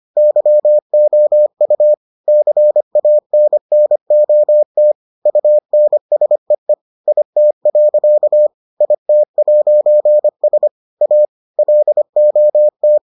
Morse code, heard as a keyed tone, about 25 wpm, 600 Hz. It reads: YOU CANNOT UNSEE IT. IT'S A LOT